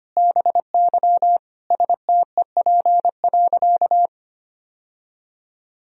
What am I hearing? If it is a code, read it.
Morse code, heard as a keyed tone, about 25 wpm, 700 Hz. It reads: BY STEP.
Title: Sentences from Top 1000 Words - Review Part 2 - 25wpm